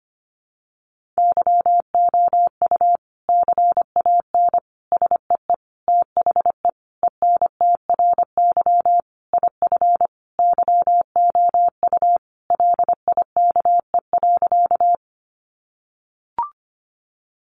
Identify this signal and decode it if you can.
Morse code, heard as a keyed tone, about 25 wpm, 700 Hz. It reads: YOU CAN SEE THE ENTRY IF YOU LIKE.
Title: Sentences from Top 1500 Words 25wpm